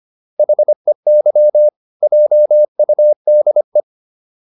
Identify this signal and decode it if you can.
Morse code, heard as a keyed tone, about 25 wpm, 600 Hz. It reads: HEY JUDE